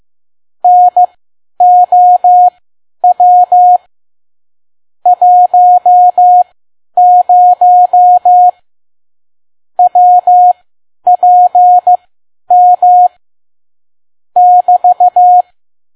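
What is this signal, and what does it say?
Morse code, heard as a keyed tone, about 15 wpm, spaced as for 10 wpm, 720 Hz. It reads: NOW 10 WPM <BT>
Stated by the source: Morse code practice at 10 WPM.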